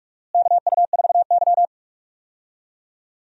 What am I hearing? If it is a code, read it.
Morse code, heard as a keyed tone, about 46 wpm, 700 Hz. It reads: KU4Y
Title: USA Call Signs - 4 Characters - Repeated 2x - 45wpm